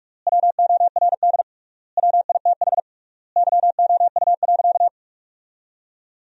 Morse code, heard as a keyed tone, about 42 wpm, 700 Hz. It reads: WORD WITH YOU.
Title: 3x Word-Spacing - Sentences from Top 300 Words - 45wpm